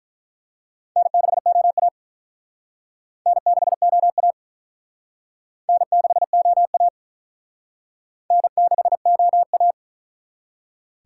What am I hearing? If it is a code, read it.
Morse code, heard as a keyed tone, about 45 wpm, 700 Hz. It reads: N6OA N6OA N6OA N6OA